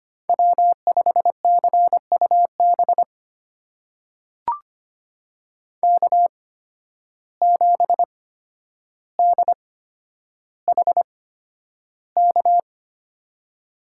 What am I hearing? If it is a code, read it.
Morse code, heard as a keyed tone, about 25 wpm, spaced as for 11 wpm, 700 Hz. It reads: W5CUB K7DHK
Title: USA Call Signs - 5 Characters - 8x Character Spacing - 25wpm